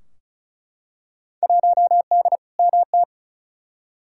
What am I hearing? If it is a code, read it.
Morse code, heard as a keyed tone, about 34 wpm, 700 Hz. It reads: 1D MT
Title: Field Day Exchanges - 35wpm